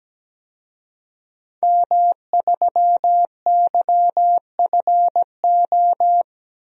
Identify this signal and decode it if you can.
Morse code, heard as a keyed tone, about 17 wpm, 700 Hz. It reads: M3YFO
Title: DXCC-100 Call Signs - 5 Characters - 17wpm